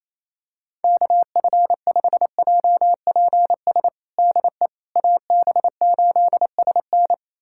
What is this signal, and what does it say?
Morse code, heard as a keyed tone, about 28 wpm, 700 Hz. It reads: KF5JPS DE AB8SN
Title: USA Callsign DE Callsign 28wpm